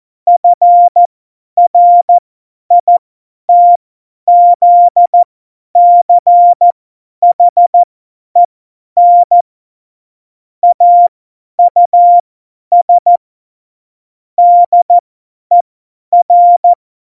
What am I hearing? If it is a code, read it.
Morse code, heard as a keyed tone, about 14 wpm, spaced as for 10 wpm, 700 Hz. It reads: FRITZCHEN AUS DER
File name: Morsekurs / Witze / Witz 28. Mai 2019, 50 F70